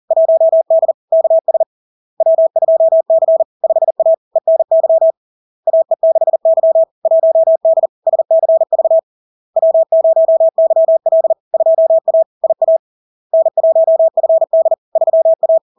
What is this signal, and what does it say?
Morse code, heard as a keyed tone, about 40 wpm, 650 Hz. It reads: 1D KS W2C 5A ENY AE6Y 1D SCV W0YL 2A IA N1FD 3A
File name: LARK-FD-2022-Log-in-CW-40wpm-650-Hz